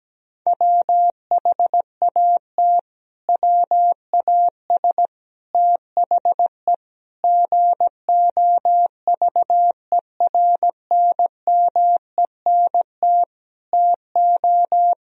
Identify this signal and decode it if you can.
Morse code, heard as a keyed tone, about 17 wpm, 700 Hz. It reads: WHAT WAS THE GOVERNMENT TO
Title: Sentences from Top 1000 Words - Review Part 3 - 17wpm